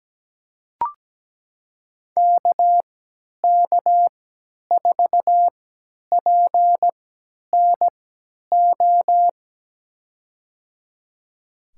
Morse code, heard as a keyed tone, about 17 wpm, spaced as for 10 wpm, 700 Hz. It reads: KK4PNO
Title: USA Call Signs - 6 Characters - 3x Character Spacing - 17wpm